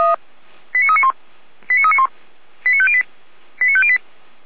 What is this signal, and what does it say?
Five-tone selective call (ZVEI), heard as quick runs of five tones, tones of about 70 ms. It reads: 89391 89391 89508 89508